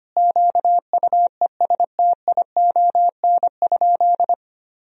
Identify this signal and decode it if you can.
Morse code, heard as a keyed tone, about 25 wpm, 700 Hz. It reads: QUESTION?
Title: Sentences from Top 400 Words and Abbreviations 25wpm